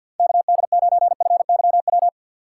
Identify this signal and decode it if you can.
Morse code, heard as a keyed tone, about 52 wpm, 700 Hz. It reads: KD9FYW
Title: USA Call Signs - 2x Character Spacing - 50wpm